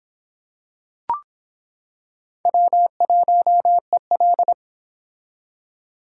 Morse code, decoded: W1EL

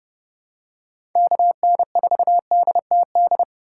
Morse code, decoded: KN4DTD

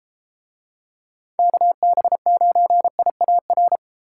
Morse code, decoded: KB9IAR

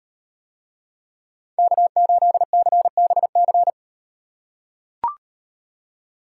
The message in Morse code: K8CBC

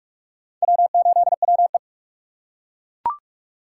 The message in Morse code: W8WE

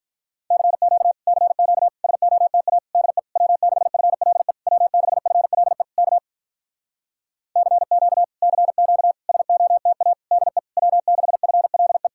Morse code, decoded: CQ CQ SOTA DE W6FLE W6FLE K CQ CQ SOTA DE W6FLE

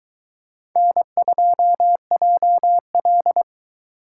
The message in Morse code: N2JL

